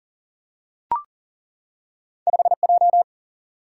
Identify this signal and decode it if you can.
Morse code, heard as a keyed tone, about 40 wpm, 700 Hz. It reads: 5 J
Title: Single-Letter Single-Number 40wpm